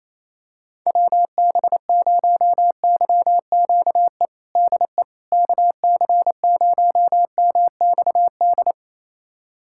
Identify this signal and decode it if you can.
Morse code, heard as a keyed tone, about 28 wpm, 700 Hz. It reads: WB0YQE DE KC0MXD
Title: USA Callsign DE Callsign - 6 Characters - 28wpm